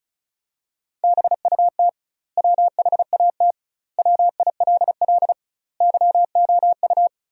Morse code, heard as a keyed tone, about 35 wpm, 700 Hz. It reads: BUT WHAT WILL YOU
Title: 2x Word-Spacing - Sentences from Top 100 Words - 35wpm